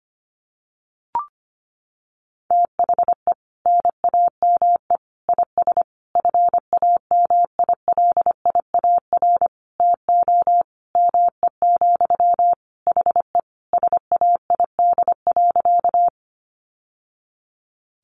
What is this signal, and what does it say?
Morse code, heard as a keyed tone, about 25 wpm, 700 Hz. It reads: THE NAME IS FAMILIAR TO ME, HE SAID.
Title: Sentences from Top 1000 Words 25wpm